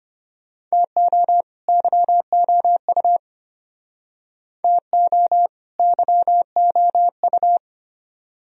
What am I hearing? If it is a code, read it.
Morse code, heard as a keyed tone, about 33 wpm, 700 Hz. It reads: TO YOU TO YOU